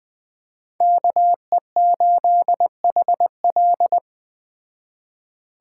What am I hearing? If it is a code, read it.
Morse code, heard as a keyed tone, about 20 wpm, 700 Hz. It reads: KE8HL